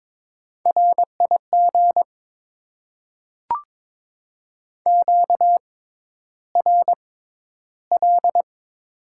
Morse code, heard as a keyed tone, about 22 wpm, spaced as for 11 wpm, 700 Hz. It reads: RIG QRL
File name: Top 20 QSO Elements - 6x Character Spacing - 22wpm